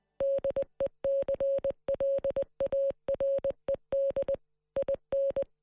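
Morse code, heard as a keyed tone, about 20 wpm, 550 Hz. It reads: DECLARED IN
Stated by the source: Morse code message re-enacting the 1916 Rising broadcast by Marconi operator David Burke, transmitted on air by 37 radio stations across the state.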